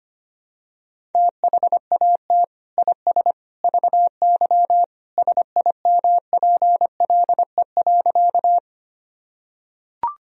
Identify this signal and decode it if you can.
Morse code, heard as a keyed tone, about 25 wpm, 700 Hz. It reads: THAT IS VY SIMPLE.
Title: Sentences from Top 400 Words and Abbreviations 25wpm